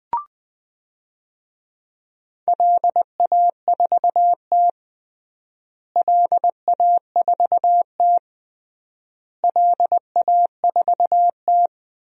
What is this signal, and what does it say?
Morse code, heard as a keyed tone, about 20 wpm, 700 Hz. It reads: LA4T LA4T LA4T
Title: DXCC-100 Call Signs - 4 Characters - Repeated 3x - 20wpm